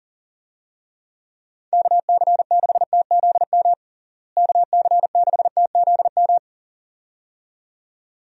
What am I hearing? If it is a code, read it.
Morse code, heard as a keyed tone, about 37 wpm, 700 Hz. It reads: KC6TZM KC6TZM
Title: USA Call Signs - 6 Characters - Repeated 2x - 40wpm